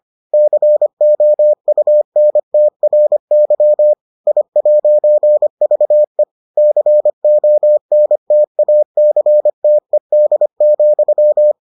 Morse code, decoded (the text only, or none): COUNTRY I'VE CONTACTED,